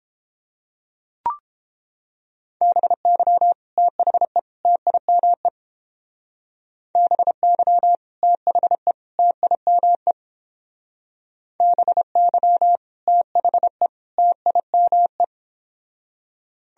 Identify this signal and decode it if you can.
Morse code, heard as a keyed tone, about 31 wpm, 700 Hz. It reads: BY THE TIME BY THE TIME BY THE TIME